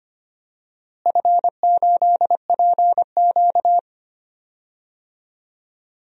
Morse code, heard as a keyed tone, about 25 wpm, 700 Hz. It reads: F8PQ
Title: DXCC-100 Call Signs - 4 Characters - Repeated 3x - 25wpm